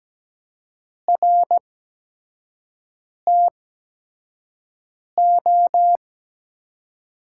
Morse code, heard as a keyed tone, about 17 wpm, spaced as for 5 wpm, 700 Hz. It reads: RTO
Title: USA Call Signs - 5 Characters - 8x Character Spacing - 17wpm